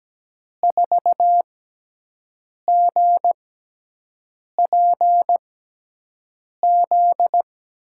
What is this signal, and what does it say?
Morse code, heard as a keyed tone, about 17 wpm, spaced as for 6 wpm, 700 Hz. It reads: 4GPZ
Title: USA Call Signs - 6x Character Spacing - 17wpm